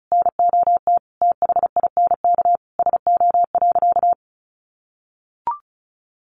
Morse code, heard as a keyed tone, about 34 wpm, 700 Hz. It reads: NOT THINK SO.
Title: Sentences from Top 1500 Words 35wpm